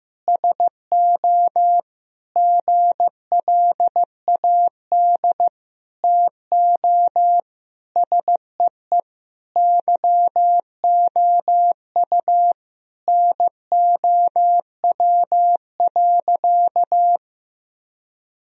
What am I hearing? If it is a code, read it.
Morse code, heard as a keyed tone, about 16 wpm, 700 Hz. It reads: SO GLAD TO SEE YOU NOW.
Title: Sentences from Top 1000 Words - Review Part 2 - 15wpm